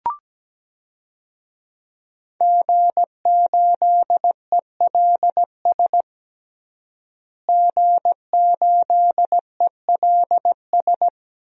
Morse code, heard as a keyed tone, about 17 wpm, 700 Hz. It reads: G8ELS G8ELS
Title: DXCC-100 Call Signs - 5 Characters - Repeated 2x - 17wpm